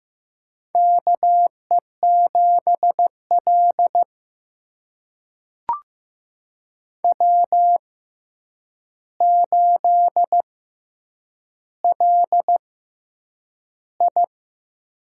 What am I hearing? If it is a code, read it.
Morse code, heard as a keyed tone, about 15 wpm, spaced as for 8 wpm, 700 Hz. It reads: KE7L W8LI
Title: USA Call Signs - 4 Characters - 6x Character Spacing - 15wpm